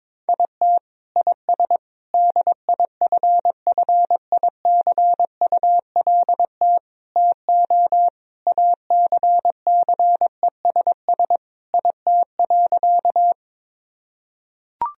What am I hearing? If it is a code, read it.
Morse code, heard as a keyed tone, about 22 wpm, 700 Hz. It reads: IT IS DIFFICULT TO ACCESS IT.
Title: Sentences from Top 1000 Words - Review Part 4 - 22wpm